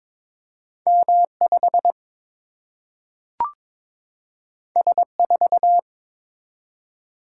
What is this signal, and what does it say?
Morse code, heard as a keyed tone, about 22 wpm, 700 Hz. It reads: M 5 S 4
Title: Single-Letter Single-Number 22wpm